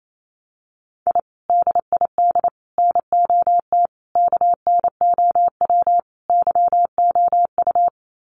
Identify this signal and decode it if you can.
Morse code, heard as a keyed tone, about 28 wpm, 700 Hz. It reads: I DID NOT KNOW YOU
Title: Sets of 5 Words Encore2 28wpm